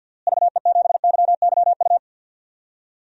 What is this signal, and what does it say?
Morse code, heard as a keyed tone, about 47 wpm, 700 Hz. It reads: VE7YYU
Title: DXCC-100 Call Signs - 6 Characters - Repeated 2x - 50wpm